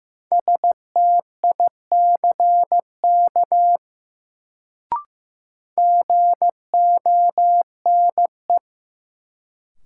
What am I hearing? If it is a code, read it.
Morse code, heard as a keyed tone, about 16 wpm, 700 Hz. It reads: STICK GONE
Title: Sentences from Top 1200 Words - New Word Drill - 15wpm